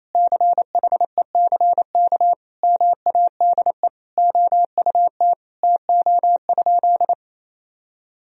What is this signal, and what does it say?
Morse code, heard as a keyed tone, about 28 wpm, 700 Hz. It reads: CHECK MADE OUT TO?
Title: Sentences from Top 1000 Words - Review Part 3 - 28wpm